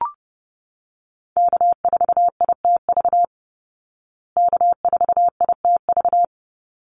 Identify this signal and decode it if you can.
Morse code, heard as a keyed tone, about 30 wpm, 700 Hz. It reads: K4ITV K4ITV
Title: Call Signs Repeated 3x - 30wpm